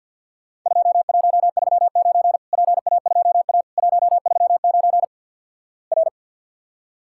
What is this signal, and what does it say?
Morse code, heard as a keyed tone, about 50 wpm, 700 Hz. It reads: 2139 PA2A 139 R
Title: Worked All Europe DX Contest - QTC Exchange with Repeats - 50wpm